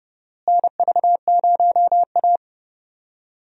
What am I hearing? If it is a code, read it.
Morse code, heard as a keyed tone, about 32 wpm, 700 Hz. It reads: NV0A
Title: USA Call Signs - 4 Characters - Repeated 2x - 30wpm